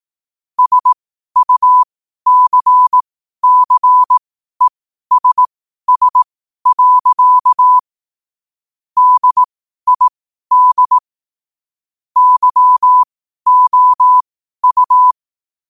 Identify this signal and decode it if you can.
Morse code, heard as a keyed tone, about 18 wpm, spaced as for 12 wpm, 1000 Hz. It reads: SUCCESS. DID YOU
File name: qotdfiles / 12wpm